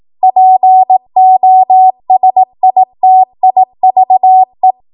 Morse code, heard as a keyed tone, about 18 wpm, 750 Hz. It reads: POSITIVE
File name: Morse / 18WPM